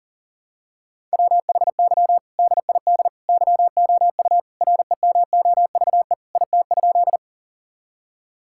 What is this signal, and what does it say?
Morse code, heard as a keyed tone, about 40 wpm, 700 Hz. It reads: WHY DID YOU REMOVE IT?
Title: Sentences from Top 1000 Words - Review Part 5 - 40wpm